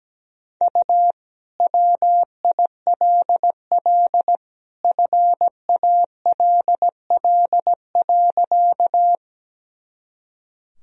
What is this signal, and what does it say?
Morse code, heard as a keyed tone, about 17 wpm, 700 Hz. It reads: U WILL FALL.
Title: Sentences from Top 500 Words and Abbreviations 17wpm